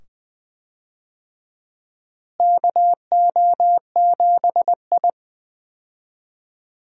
Morse code, heard as a keyed tone, about 20 wpm, 700 Hz. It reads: KO7I
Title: USA Call Signs - 4 Characters - Repeated 2x - 20wpm